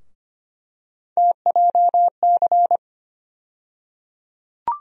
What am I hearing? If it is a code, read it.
Morse code, heard as a keyed tone, about 25 wpm, 700 Hz. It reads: TJC
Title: USA Call Sign Postfixes - 3 Letters 25wpm